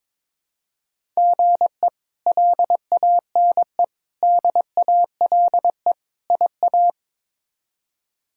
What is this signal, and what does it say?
Morse code, heard as a keyed tone, about 21 wpm, 700 Hz. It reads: GE LANE DALE IA
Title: K1USN SST QSO Practice - 22wpm